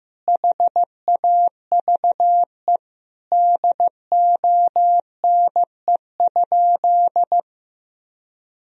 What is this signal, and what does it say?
Morse code, heard as a keyed tone, about 16 wpm, 700 Hz. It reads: HAVE DONE?
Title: Sentences from Top 1000 Words - Review Part 3 - 15wpm